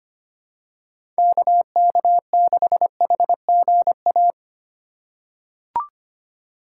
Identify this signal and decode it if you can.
Morse code, heard as a keyed tone, about 25 wpm, 700 Hz. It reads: KK6HGA
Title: USA Call Signs - 3x Character Spacing - 25wpm